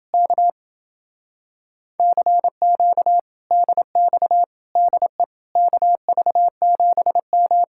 Morse code, heard as a keyed tone, about 29 wpm, 700 Hz. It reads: K CQ DX DE KV7M